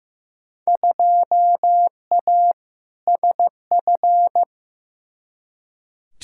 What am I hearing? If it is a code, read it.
Morse code, heard as a keyed tone, about 15 wpm, 700 Hz. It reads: 2A SF
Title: Field Day Exchanges - 15wpm